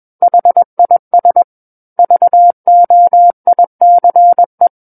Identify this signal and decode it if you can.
Morse code, heard as a keyed tone, about 21 wpm, 700 Hz. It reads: HIS VOICE
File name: Book One / 12-Chapter 12 - 21 WPM